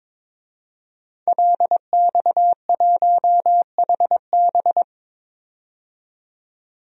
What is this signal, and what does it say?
Morse code, heard as a keyed tone, about 22 wpm, 700 Hz. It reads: LX1HB
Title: DXCC-100 Call Signs - 5 Characters - 22wpm